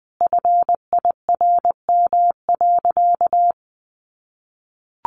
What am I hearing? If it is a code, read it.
Morse code, heard as a keyed tone, about 20 wpm, 700 Hz. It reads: FIRM.
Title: Sentences from Top 1400 Words 20wpm